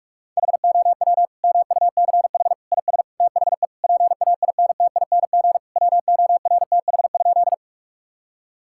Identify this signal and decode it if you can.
Morse code, heard as a keyed tone, about 46 wpm, 700 Hz. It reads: HOW MUCH IS THE PAINTING WORTH?
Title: Sentences from Top 1000 Words - Review Part 6 - 45wpm